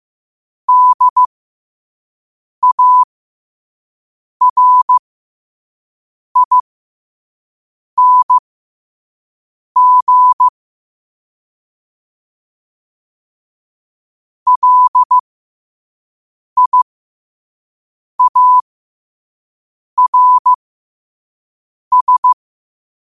Morse code, heard as a keyed tone, about 15 wpm, spaced as for 5 wpm, 1000 Hz. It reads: DARING LIARS
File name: qotdfiles / 5wpm